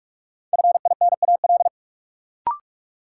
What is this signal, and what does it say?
Morse code, heard as a keyed tone, about 42 wpm, 700 Hz. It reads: FINAL
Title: Sentences from Top 800 Words - New Word Drill - 45wpm